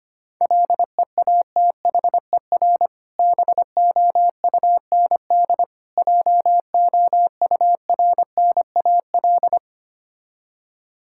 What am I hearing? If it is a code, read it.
Morse code, heard as a keyed tone, about 25 wpm, 690 Hz. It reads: LEATHER BOUND JOURNAL
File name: practice / 25wpm / Difficult Words with Hints - 25wpm